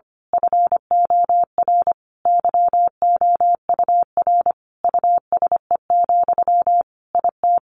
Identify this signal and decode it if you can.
Morse code, heard as a keyed tone, about 25 wpm, 700 Hz. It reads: FOR YOUR USE, IT
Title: A Sound Of Thunder 25wpm